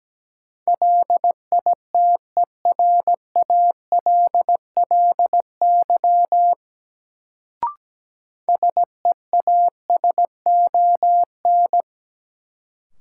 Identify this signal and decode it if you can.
Morse code, heard as a keyed tone, about 17 wpm, 700 Hz. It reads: LITERALLY SEASON
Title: Sentences from Top 1400 Words - New Word Drill - 17wpm